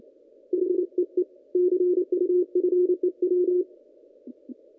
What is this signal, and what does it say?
Morse code, heard as a keyed tone, about 29 wpm, 360 Hz. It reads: HEE CU FEW
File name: contests / ss-cw-k5tr-op-n5rz -1800